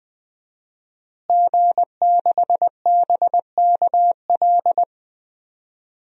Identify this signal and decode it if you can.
Morse code, heard as a keyed tone, about 20 wpm, 700 Hz. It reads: G6BKL
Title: DXCC-100 Call Signs - 5 Characters - Repeated 3x - 20wpm